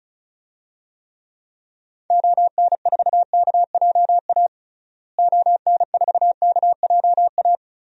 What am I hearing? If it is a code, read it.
Morse code, heard as a keyed tone, about 36 wpm, 700 Hz. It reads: ON4KJA ON4KJA
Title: DXCC-100 Call Signs - 6 Characters - Repeated 2x - 35wpm